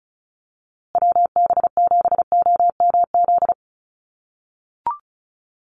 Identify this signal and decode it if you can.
Morse code, heard as a keyed tone, about 34 wpm, 700 Hz. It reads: WB7OMZ
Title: USA Call Signs Repeated 2x 35wpm